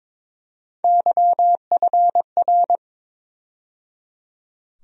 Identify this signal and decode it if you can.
Morse code, heard as a keyed tone, about 22 wpm, 700 Hz. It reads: YFR
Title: USA Call Sign Postfixes - 3 Letters 22wpm